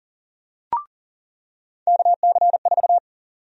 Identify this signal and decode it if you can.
Morse code, heard as a keyed tone, about 37 wpm, 700 Hz. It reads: KC4
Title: USA Call Sign Prefixes - 40wpm